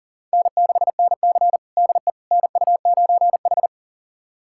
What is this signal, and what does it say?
Morse code, heard as a keyed tone, about 40 wpm, 700 Hz. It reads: N6NC DE NU9H
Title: USA Callsign DE Callsign - 4 Characters - 40wpm